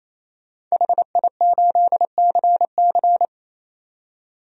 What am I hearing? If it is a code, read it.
Morse code, heard as a keyed tone, about 27 wpm, 700 Hz. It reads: HI8CC
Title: DXCC-100 Call Signs - 5 Characters - 28wpm